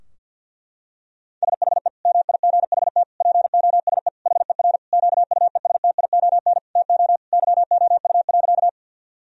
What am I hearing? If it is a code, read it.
Morse code, heard as a keyed tone, about 50 wpm, 700 Hz. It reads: SHE MIGHT POSE HER QUESTION TO YOU.
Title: Sentences from Top 1000 Words - Review Part 2 - 50wpm